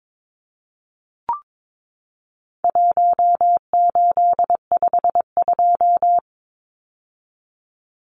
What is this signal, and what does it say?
Morse code, heard as a keyed tone, about 22 wpm, 700 Hz. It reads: 1852
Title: Numbers - 4-Digit 22wpm